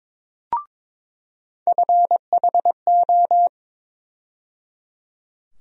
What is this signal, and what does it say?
Morse code, heard as a keyed tone, about 22 wpm, 700 Hz. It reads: FHO
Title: USA Call Sign Postfixes - 3 Letters 22wpm